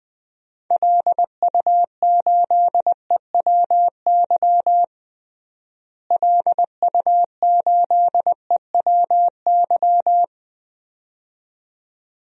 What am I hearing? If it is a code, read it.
Morse code, heard as a keyed tone, about 21 wpm, 700 Hz. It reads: LU8EWY LU8EWY
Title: DXCC-100 Call Signs - 6 Characters - Repeated 2x - 20wpm